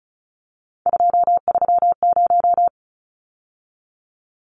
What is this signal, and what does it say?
Morse code, heard as a keyed tone, about 34 wpm, 700 Hz. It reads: 230
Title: Numbers - 3-Digit 35wpm